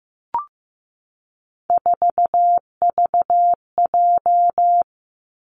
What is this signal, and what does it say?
Morse code, heard as a keyed tone, about 15 wpm, 700 Hz. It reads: 4VJ
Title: Three Challenging Letter-Number Combinations 15wpm